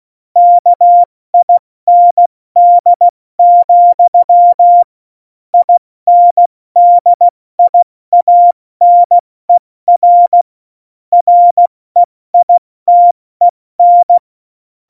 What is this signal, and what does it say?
Morse code, heard as a keyed tone, about 16 wpm, 700 Hz. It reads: KIND, INDIANER REITEN
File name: Morsekurs / Witze / Witz 11 August 2020, 70 F80